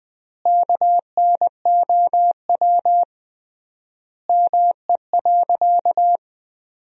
Morse code, heard as a keyed tone, about 20 wpm, 700 Hz. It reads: KNOW ME.
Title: 3x Word-Spacing - Sentences from Top 200 Words - 20wpm